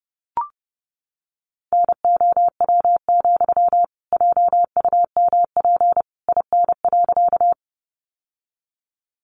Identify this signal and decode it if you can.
Morse code, heard as a keyed tone, about 30 wpm, 700 Hz. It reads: NOW, JUMP IN.
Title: Sentences from Top 1300 Words 30wpm